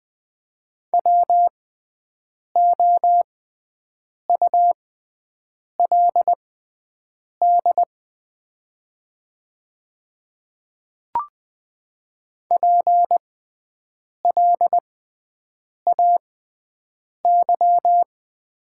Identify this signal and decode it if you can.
Morse code, heard as a keyed tone, about 20 wpm, spaced as for 7 wpm, 700 Hz. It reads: WOULD PLAY
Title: 6x Character Spacing - Top 200 Words - 20wpm